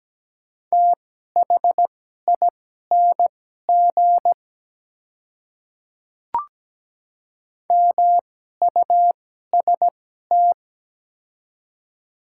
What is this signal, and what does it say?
Morse code, heard as a keyed tone, about 17 wpm, spaced as for 12 wpm, 700 Hz. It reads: THING MUST
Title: 2x Character Spacing - Top 200 Words - 17wpm